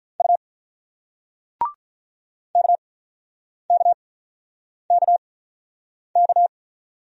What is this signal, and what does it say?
Morse code, heard as a keyed tone, about 47 wpm, spaced as for 9 wpm, 700 Hz. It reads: U KKKK